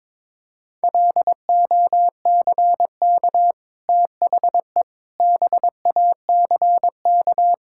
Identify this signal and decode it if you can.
Morse code, heard as a keyed tone, about 22 wpm, 700 Hz. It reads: LOCK THE BACK
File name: practice / 22wpm / Easy Sentences with Hints - 22wpm